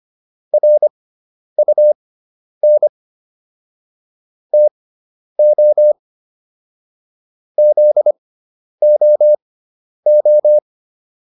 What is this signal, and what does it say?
Morse code, heard as a keyed tone, about 25 wpm, spaced as for 10 wpm, 600 Hz. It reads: RUN TO ZOO